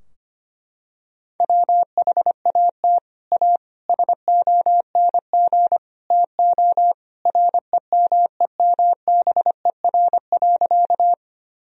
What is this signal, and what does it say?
Morse code, heard as a keyed tone, about 25 wpm, 700 Hz. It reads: WHAT A SONG TO REMEMBER.
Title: Sentences from Top 1000 Words - Review Part 2 - 25wpm